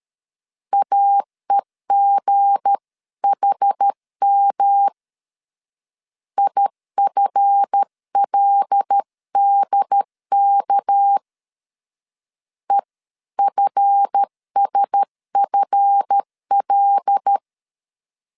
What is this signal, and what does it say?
Morse code, heard as a keyed tone, about 13 wpm, 800 Hz. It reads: AEGHM IFLDK EFSFL
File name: LEZIONI / 7 GF9Z / 7 GF 12wpm 2122